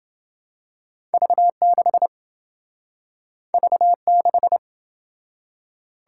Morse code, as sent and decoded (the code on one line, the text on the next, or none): ...- -.... / ...- -....
V6 V6